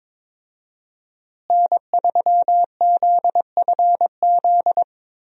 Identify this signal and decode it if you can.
Morse code, heard as a keyed tone, about 22 wpm, 700 Hz. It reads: N3ZFZ
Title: USA Call Signs - 5 Characters - Repeated 2x - 22wpm